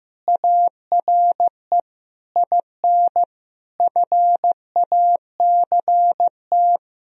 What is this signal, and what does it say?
Morse code, heard as a keyed tone, about 15 wpm, 700 Hz. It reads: ARE IN FACT
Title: Sentences from Top 1000 Words - Review Part 4 - 15wpm